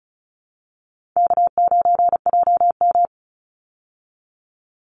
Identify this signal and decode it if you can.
Morse code, heard as a keyed tone, about 34 wpm, 700 Hz. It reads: K9JM
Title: Call Signs Repeated 2x - 35wpm